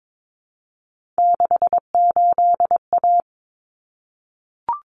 Morse code, heard as a keyed tone, about 22 wpm, 700 Hz. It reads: 68A
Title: Common Cut Numbers - 3-Digit 22wpm